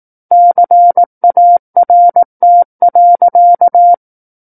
Morse code, heard as a keyed tone, about 18 wpm, 700 Hz. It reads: CART.
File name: 18 WPM